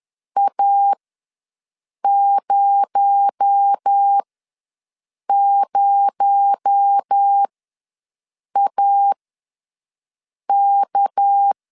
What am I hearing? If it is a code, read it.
Morse code, heard as a keyed tone, about 10 wpm, spaced as for 5 wpm, 800 Hz. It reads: A00AK
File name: LEZIONI / 6 PKX0 / 6 PKX0 ws 10wpm 4220